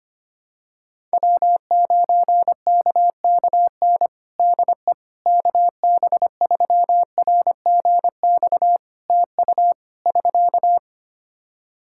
Morse code, heard as a keyed tone, about 25 wpm, 700 Hz. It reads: W9KKN DE KB3RGX TU <SK>
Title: Practice QSO - Exchange 4 - Ending - 25wpm